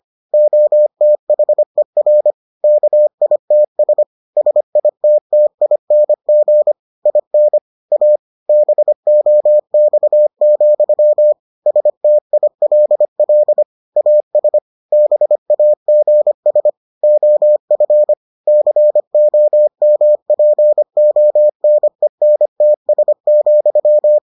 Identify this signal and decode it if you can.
Morse code, heard as a keyed tone, about 25 wpm, 600 Hz. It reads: OTHER KITS SITTING IN A BOX, STILL AS BAGS OF COMPONENTS,